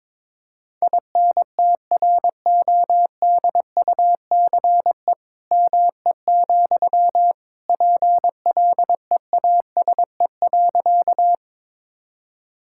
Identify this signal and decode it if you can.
Morse code, heard as a keyed tone, about 22 wpm, 700 Hz. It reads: INTRODUCE ME, PLEASE.
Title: Sentences from Top 1600 Words 22wpm